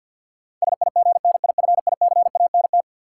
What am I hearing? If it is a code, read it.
Morse code, heard as a keyed tone, about 53 wpm, 700 Hz. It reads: SIGNIFICANT
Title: Sentences from Top 1000 Words - Review Part 6 - New Word Drill - 50wpm